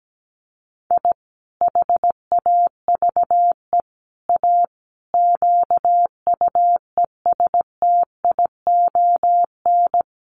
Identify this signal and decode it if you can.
Morse code, heard as a keyed tone, about 17 wpm, 700 Hz. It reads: I HAVE A QUESTION
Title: Sets of 5 Words Encore2 17wpm